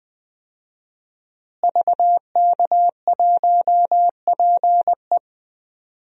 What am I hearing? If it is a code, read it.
Morse code, heard as a keyed tone, about 20 wpm, 700 Hz. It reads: VK1PE
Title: DXCC-100 Call Signs - 5 Characters - Repeated 3x - 20wpm